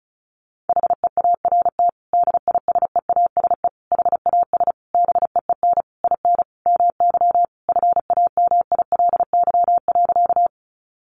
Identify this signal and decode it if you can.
Morse code, heard as a keyed tone, about 34 wpm, 700 Hz. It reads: HEART DISEASE HAS BEEN IN MY FAMILY.
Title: Sentences from Top 1200 Words 35wpm